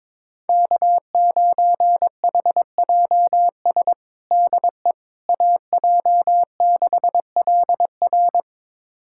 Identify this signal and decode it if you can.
Morse code, heard as a keyed tone, about 21 wpm, 700 Hz. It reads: K9HJS DE AJ6LR
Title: USA Callsign DE Callsign - 5 Characters - 22wpm